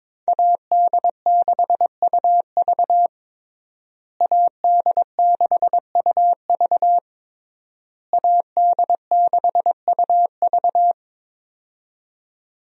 Morse code, heard as a keyed tone, about 22 wpm, 700 Hz. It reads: AD6UV AD6UV AD6UV
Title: USA Call Signs - 5 Characters - Repeated 3x - 22wpm